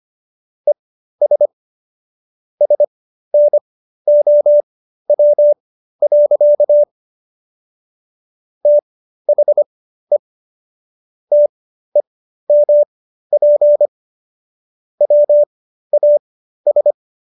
Morse code, heard as a keyed tone, about 25 wpm, spaced as for 13 wpm, 590 Hz. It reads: ES SNOW. THE TEMP WAS